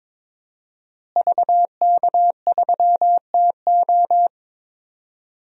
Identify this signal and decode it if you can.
Morse code, heard as a keyed tone, about 22 wpm, 700 Hz. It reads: VK3TO
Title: DXCC-100 Call Signs - 5 Characters - Repeated 3x - 22wpm